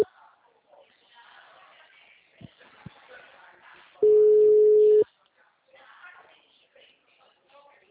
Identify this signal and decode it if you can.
Ringback tone (Europe, 425 Hz), with speech.
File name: out- -2013- - - .30